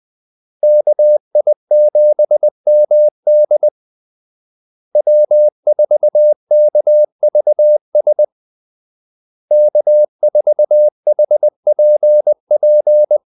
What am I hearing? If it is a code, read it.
Morse code, heard as a keyed tone, about 20 wpm, 600 Hz. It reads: KI7MD W4KVS K4HPP
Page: Kaldesignaler 20 wpm